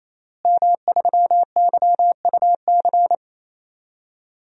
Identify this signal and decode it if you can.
Morse code, heard as a keyed tone, about 28 wpm, 700 Hz. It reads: M3YUC
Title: DXCC-100 Call Signs - 5 Characters - Repeated 3x - 28wpm